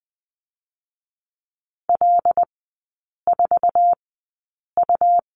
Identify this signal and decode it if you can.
Morse code, heard as a keyed tone, about 20 wpm, spaced as for 8 wpm, 700 Hz. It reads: L4U
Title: 127 Lesson 21e - Number 6 - Three Characters 20wpm